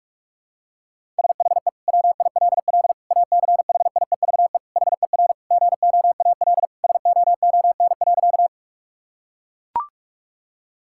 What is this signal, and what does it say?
Morse code, heard as a keyed tone, about 42 wpm, 700 Hz. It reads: SHE WILL ACHIEVE HER GOAL SOON.
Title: Sentences from Top 1000 Words - Review Part 6 - 45wpm